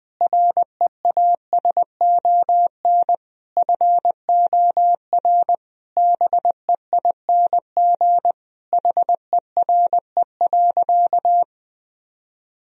Morse code, heard as a keyed tone, about 21 wpm, 700 Hz. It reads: REASON FOR BEING HERE.
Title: Sentences from Top 1000 Words - Review Part 2 - 20wpm